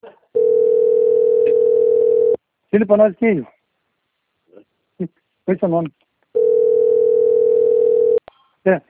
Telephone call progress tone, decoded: ringback tone

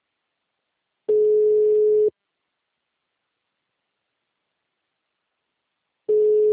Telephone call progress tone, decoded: ringback tone

